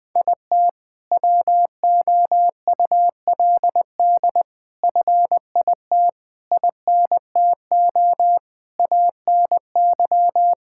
Morse code, heard as a keyed tone, about 20 wpm, 700 Hz. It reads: IT WOULD FIT INTO ANY
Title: Sentences from Top 1000 Words - Review Part 5 - 20wpm